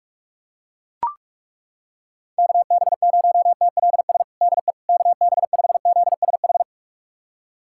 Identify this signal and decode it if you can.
Morse code, heard as a keyed tone, about 42 wpm, 700 Hz. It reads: KB0TLS DE KB5ZSH
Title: USA Callsign DE Callsign - 6 Characters - 45wpm